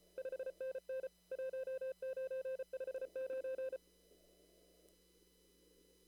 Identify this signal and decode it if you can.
Morse code, heard as a keyed tone, about 34 wpm, 520 Hz. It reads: 5NN 1959
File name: CW / kt5j wpx cw-op-k5pi